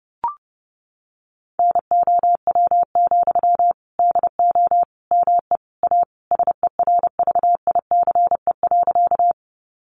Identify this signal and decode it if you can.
Morse code, heard as a keyed tone, about 30 wpm, 700 Hz. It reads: NOW, DO ME A SERVICE.
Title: Sentences from Top 700 Words 30wpm